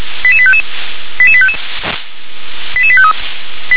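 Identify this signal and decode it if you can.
Five-tone selective call (ZVEI), heard as quick runs of five tones, tones of about 70 ms. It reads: 88855 88855 88853
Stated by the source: Received by SDR, ZVEI five-tone selective call 88740.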